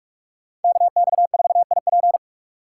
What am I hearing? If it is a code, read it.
Morse code, heard as a keyed tone, about 45 wpm, 700 Hz. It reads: KX4IP